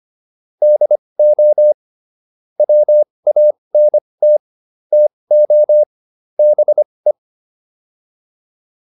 Morse code, heard as a keyed tone, about 25 wpm, spaced as for 20 wpm, 600 Hz. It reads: DO WANT TO BE